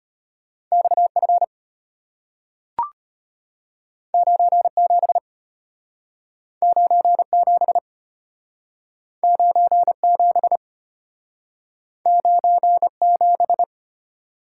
Morse code, heard as a keyed tone, about 33 wpm, 700 Hz. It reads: XF 97 97 97 97